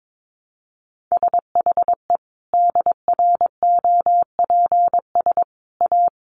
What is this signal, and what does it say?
Morse code, heard as a keyed tone, about 22 wpm, 700 Hz. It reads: SHE DROPS A
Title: Additional Easy Phrases 22wpm